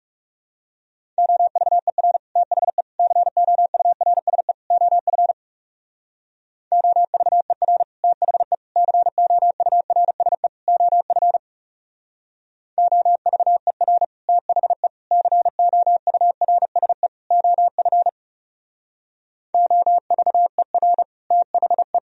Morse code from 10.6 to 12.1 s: OF

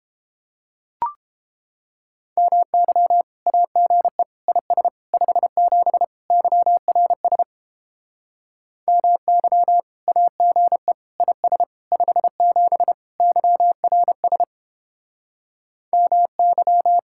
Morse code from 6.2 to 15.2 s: YRS MY AGE IS 57 YRS

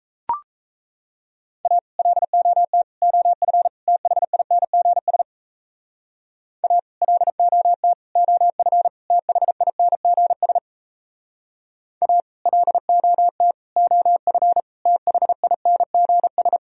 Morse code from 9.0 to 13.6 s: THINGS A LOT

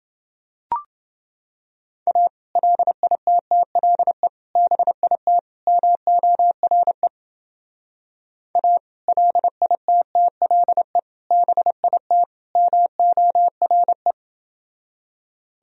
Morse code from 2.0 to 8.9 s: A LITTLE BIT MORE A

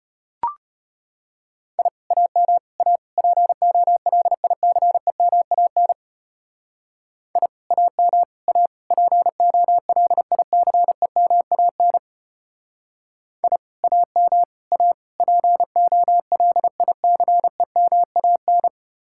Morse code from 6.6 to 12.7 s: I AM A POLICEMAN

